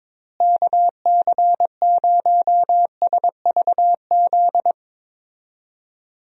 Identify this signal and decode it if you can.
Morse code, heard as a keyed tone, about 21 wpm, 700 Hz. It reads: KC0SVZ
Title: USA Call Signs - 6 Characters - Repeated 3x - 22wpm